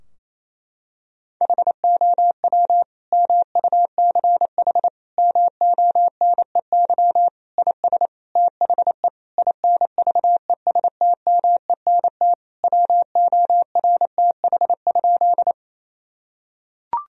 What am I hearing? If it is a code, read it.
Morse code, heard as a keyed tone, about 28 wpm, 700 Hz. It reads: HOW MUCH MONEY IS THE INVESTMENT WORTH?
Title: Sentences from Top 1000 Words - Review Part 4 - 28wpm